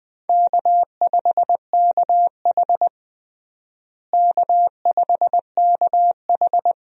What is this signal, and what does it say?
Morse code, heard as a keyed tone, about 20 wpm, 700 Hz. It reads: K5KH K5KH
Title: USA Call Signs - 4 Characters - Repeated 2x - 20wpm